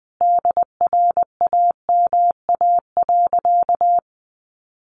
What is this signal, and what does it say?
Morse code, heard as a keyed tone, about 20 wpm, 700 Hz. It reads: DRAMA.
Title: Sentences from Top 1500 Words 20wpm